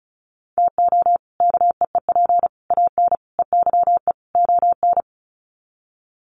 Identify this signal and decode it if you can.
Morse code, heard as a keyed tone, about 34 wpm, 700 Hz. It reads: TO KEEP AN EYE ON
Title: Sets of 5 Words Encore 35wpm